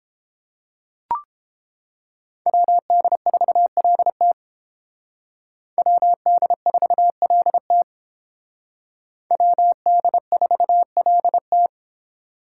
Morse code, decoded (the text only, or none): WD4LT WD4LT WD4LT